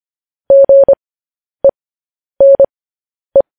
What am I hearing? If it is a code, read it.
Morse code, heard as a keyed tone, about 25 wpm, spaced as for 10 wpm, 550 Hz. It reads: GENE
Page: News Headlines in Morse Code at 10 WPM